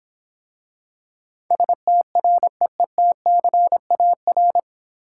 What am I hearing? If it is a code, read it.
Morse code, heard as a keyed tone, about 26 wpm, 700 Hz. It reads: STREETCAR